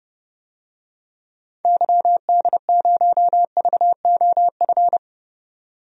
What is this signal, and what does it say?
Morse code, heard as a keyed tone, about 30 wpm, 700 Hz. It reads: YD0VOF
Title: DXCC-100 Call Signs - 6 Characters - 30wpm